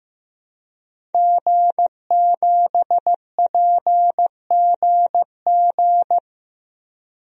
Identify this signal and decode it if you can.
Morse code, heard as a keyed tone, about 15 wpm, 700 Hz. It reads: G7PGG
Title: DXCC-100 Call Signs - 5 Characters - 15wpm